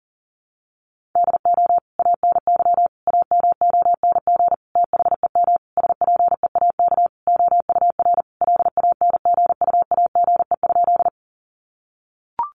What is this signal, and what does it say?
Morse code, heard as a keyed tone, about 40 wpm, 700 Hz. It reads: DO ANY AMONG THEM SPEAK OUR LANGUAGE?
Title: Sentences from Top 1100 Words 40wpm